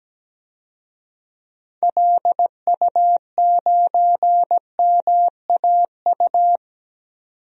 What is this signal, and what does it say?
Morse code, heard as a keyed tone, about 17 wpm, 700 Hz. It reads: LU9MAU
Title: DXCC-100 Call Signs - 6 Characters - 17wpm